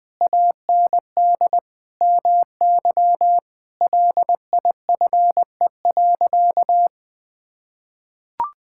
Morse code, heard as a keyed tone, about 20 wpm, 700 Hz. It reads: AND MY LIFE.
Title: Sentences from Top 1000 Words - Review Part 5 - 20wpm